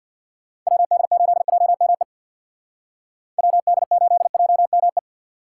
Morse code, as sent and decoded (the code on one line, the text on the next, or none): .-- -.. ---.. .--- -- . / .-- -.. ---.. .--- -- .
WD8JME WD8JME